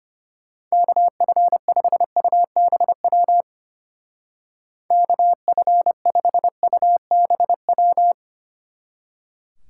Morse code, heard as a keyed tone, about 31 wpm, 700 Hz. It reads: KF5UBW KF5UBW
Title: Speed Racing - USA 2x3 Call Signs 25wpm